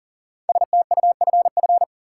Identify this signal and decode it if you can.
Morse code, heard as a keyed tone, about 40 wpm, 700 Hz. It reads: STUFF
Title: Sentences from Top 1400 Words - New Word Drill - 40wpm